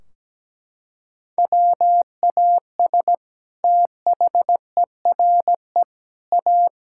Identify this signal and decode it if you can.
Morse code, heard as a keyed tone, about 17 wpm, 700 Hz. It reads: WAS THERE A
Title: Sentences from Top 1600 Words 17wpm